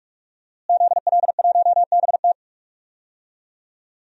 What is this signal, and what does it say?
Morse code, heard as a keyed tone, about 46 wpm, 700 Hz. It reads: ZL1BT
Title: DXCC-100 Call Signs - 5 Characters - Repeated 2x - 45wpm